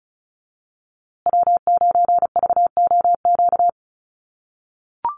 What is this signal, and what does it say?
Morse code, heard as a keyed tone, about 34 wpm, 700 Hz. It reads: W9VOQ
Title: USA Call Signs Repeated 3x 35wpm